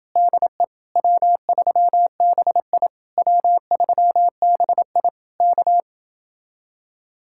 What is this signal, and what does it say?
Morse code, heard as a keyed tone, about 29 wpm, 700 Hz. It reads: DE W3BI W3BI K